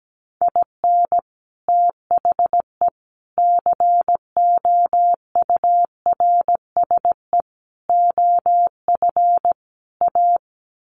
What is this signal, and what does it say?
Morse code, heard as a keyed tone, about 17 wpm, 700 Hz. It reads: IN THE COURSE OF A
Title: Sets of 5 Words Encore2 17wpm